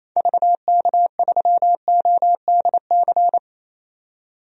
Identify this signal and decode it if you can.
Morse code, heard as a keyed tone, about 28 wpm, 700 Hz. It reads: VK3ODC
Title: DXCC-100 Call Signs - 6 Characters - 28wpm